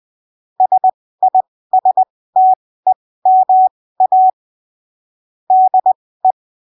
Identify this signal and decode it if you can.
Morse code, heard as a keyed tone, about 20 wpm, spaced as for 15 wpm, 750 Hz. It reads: SISTEMA DE
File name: arquivos / texto / texto 22 20ppm